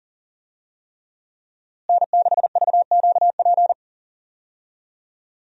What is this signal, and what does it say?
Morse code, heard as a keyed tone, about 39 wpm, 700 Hz. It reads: N6VQP
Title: USA Call Signs - 5 Characters - Repeated 3x - 40wpm